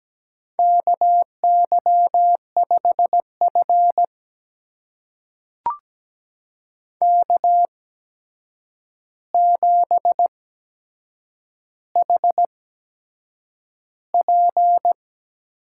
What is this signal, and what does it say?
Morse code, heard as a keyed tone, about 17 wpm, spaced as for 7 wpm, 700 Hz. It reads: KY5F K7HP
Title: USA Call Signs - 4 Characters - 8x Character Spacing - 17wpm